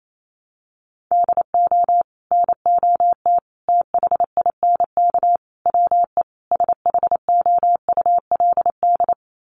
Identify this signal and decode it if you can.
Morse code, heard as a keyed tone, about 28 wpm, 700 Hz. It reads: DO NOT THINK WE SHOULD
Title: Sets of 5 Words Encore 28wpm